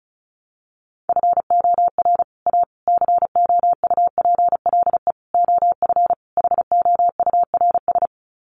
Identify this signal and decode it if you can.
Morse code, heard as a keyed tone, about 34 wpm, 700 Hz. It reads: FOR A COUPLE OF HOURS
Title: Sets of 5 Words Encore2 35wpm